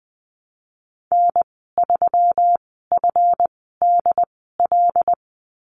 Morse code, heard as a keyed tone, about 20 wpm, spaced as for 15 wpm, 700 Hz. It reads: N3FDL
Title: 186 Lesson 30g - Letter b - Call Signs 20wpm